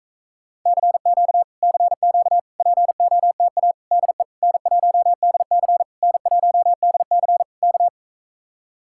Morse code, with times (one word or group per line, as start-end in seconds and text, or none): 0.7-1.4 CQ
1.6-2.4 CQ
2.6-3.7 POTA
3.9-4.2 DE
4.4-5.8 N1DC
6.0-7.4 N1DC
7.6-7.9 K